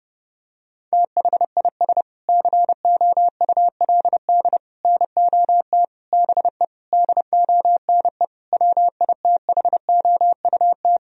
Morse code, heard as a keyed tone, about 30 wpm, 700 Hz. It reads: THIS COULD NOT BE DONE WITHOUT
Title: Sentences from Top 1000 Words - Review Part 3 - 30wpm